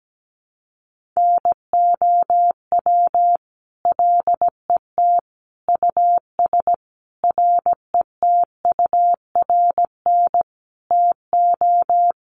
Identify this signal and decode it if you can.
Morse code, heard as a keyed tone, about 17 wpm, 700 Hz. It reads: NOW LET US RETURN TO
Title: Sentences from Top 800 Words 17wpm